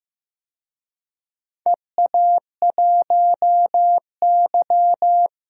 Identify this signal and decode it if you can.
Morse code, heard as a keyed tone, about 15 wpm, 700 Hz. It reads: EA1Y
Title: DXCC-100 Call Signs - 4 Characters - 15wpm